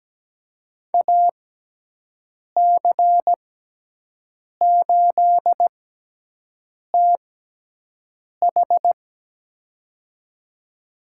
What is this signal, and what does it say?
Morse code, heard as a keyed tone, about 17 wpm, spaced as for 6 wpm, 700 Hz. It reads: AC8TH
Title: USA Call Signs - 5 Characters - 6x Character Spacing - 17wpm